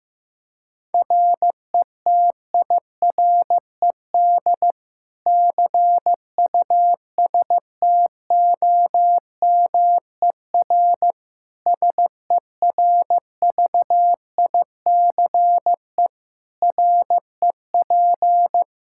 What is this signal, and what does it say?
Morse code, heard as a keyed tone, about 16 wpm, 700 Hz. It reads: RETIRED CUSTOMER SERVICE REP
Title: Practice QSO - Exchange 3 - More Info - 15wpm